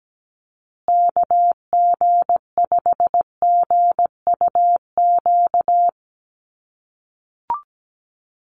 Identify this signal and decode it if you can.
Morse code, heard as a keyed tone, about 17 wpm, 700 Hz. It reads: KG5GUQ
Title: USA Call Signs Repeated 3x 17wpm